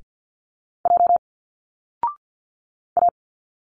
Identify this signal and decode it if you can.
Morse code, heard as a keyed tone, about 50 wpm, 700 Hz. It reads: J A
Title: Single Letters 50wpm